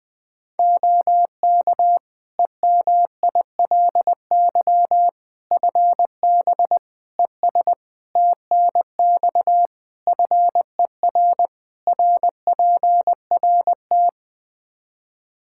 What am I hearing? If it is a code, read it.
Morse code, heard as a keyed tone, about 21 wpm, 700 Hz. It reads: OK EMILY FB ES TNX FER RPRT
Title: Practice QSO - Exchange 2 - Info - 20wpm